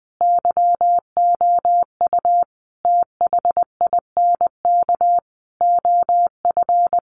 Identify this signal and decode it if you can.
Morse code, heard as a keyed tone, about 20 wpm, 700 Hz. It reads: YOU THINK OF
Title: Sentences from Top 1200 Words 20wpm